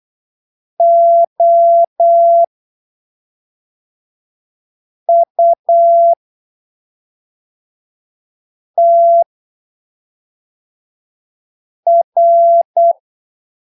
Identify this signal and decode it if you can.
Morse code, heard as a keyed tone, about 8 wpm, spaced as for 3 wpm, 680 Hz. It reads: OUTR
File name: arquivos / texto / texto 19 08ppm